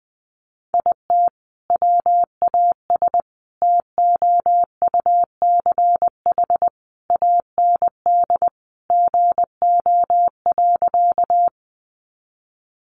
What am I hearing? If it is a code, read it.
Morse code, heard as a keyed tone, about 20 wpm, 700 Hz. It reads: IT WAS TOUCH AND GO.
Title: Sentences from Top 1200 Words 20wpm